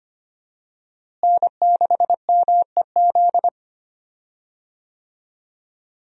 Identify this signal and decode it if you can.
Morse code, heard as a keyed tone, about 25 wpm, 700 Hz. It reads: N6MEZ
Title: USA Call Signs - 5 Characters - Repeated 2x - 25wpm